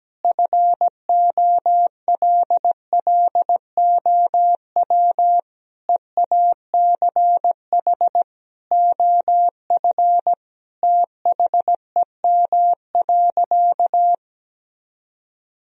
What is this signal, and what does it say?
Morse code, heard as a keyed tone, about 17 wpm, 700 Hz. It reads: FOLLOW EACH OF THEM.
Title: Sentences from Top 1000 Words - Review Part 1 - 17wpm